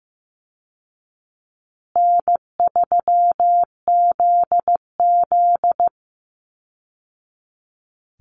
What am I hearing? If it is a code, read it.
Morse code, heard as a keyed tone, about 15 wpm, 700 Hz. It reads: N3ZZ
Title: Easy Call Signs Repeated 2x - 15wpm